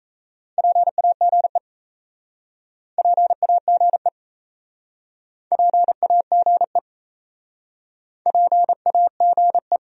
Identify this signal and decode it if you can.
Morse code, heard as a keyed tone, about 37 wpm, 700 Hz. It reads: PAGE PAGE PAGE PAGE